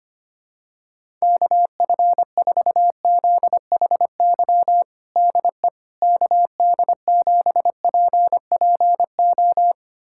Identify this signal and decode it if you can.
Morse code, heard as a keyed tone, about 25 wpm, 700 Hz. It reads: KF4ZHY DE KD7PPO
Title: USA Callsign DE Callsign 25wpm